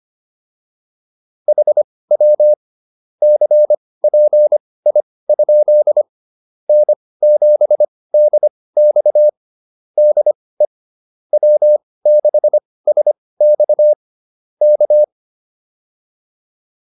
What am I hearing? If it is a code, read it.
Morse code, heard as a keyed tone, about 25 wpm, spaced as for 18 wpm, 600 Hz. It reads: HW CPI? N7DX DE W6SX K